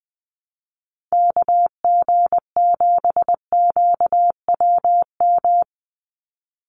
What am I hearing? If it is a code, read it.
Morse code, heard as a keyed tone, about 20 wpm, 700 Hz. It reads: KG7QWM
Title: USA Call Signs 20wpm